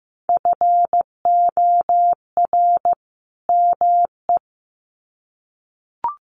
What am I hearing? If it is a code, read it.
Morse code, heard as a keyed tone, about 15 wpm, 700 Hz. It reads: FOR ME
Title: Sets of 2 Words 15wpm